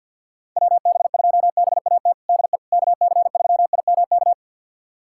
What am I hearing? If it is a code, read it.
Morse code, heard as a keyed tone, about 51 wpm, 700 Hz. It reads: WB2BAT DE KC3IMK
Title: USA Callsign DE Callsign - 6 Characters - 50wpm